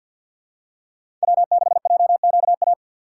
Morse code, heard as a keyed tone, about 47 wpm, 700 Hz. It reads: W6JQA
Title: USA Call Signs - 6x Character Spacing - 50wpm